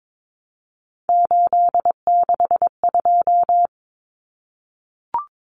Morse code, heard as a keyed tone, about 22 wpm, 700 Hz. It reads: 862
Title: Numbers - 3-Digit 22wpm